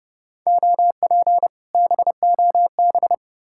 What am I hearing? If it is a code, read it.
Morse code, heard as a keyed tone, about 37 wpm, 700 Hz. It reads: OP BOB
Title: Speed-Racing Common QSO Phrases - 30wpm